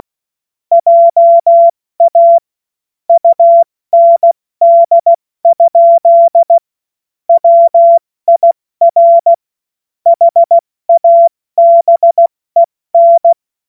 Morse code, heard as a keyed tone, about 16 wpm, spaced as for 14 wpm, 680 Hz. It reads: JA UND? WIR HABEN
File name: Morsekurs / Witze / Witz 02. Februar 2021, 70 F80